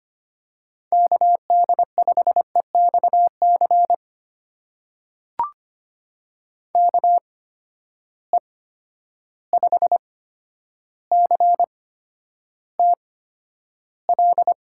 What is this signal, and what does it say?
Morse code, heard as a keyed tone, about 25 wpm, spaced as for 11 wpm, 700 Hz. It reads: KD5EXC KE5CTL
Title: USA Call Signs - 6 Characters - 8x Character Spacing - 25wpm